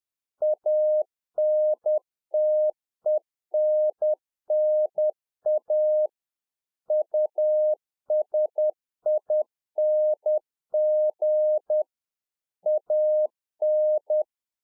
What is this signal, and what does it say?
Morse code, heard as a keyed tone, about 10 wpm, 610 Hz. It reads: ANTENNA USING AN
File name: UNFOGETTABLE QSO 10WPM